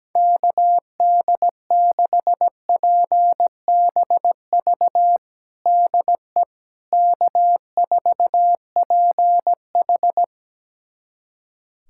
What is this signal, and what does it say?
Morse code, heard as a keyed tone, about 17 wpm, 700 Hz. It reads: KD6PBV DE K4PH
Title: USA Callsign DE Callsign 17wpm